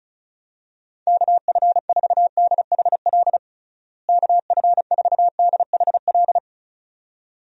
Morse code, heard as a keyed tone, about 34 wpm, 700 Hz. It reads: KF4DHL KF4DHL
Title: USA Call Signs - 6 Characters - Repeated 2x - 35wpm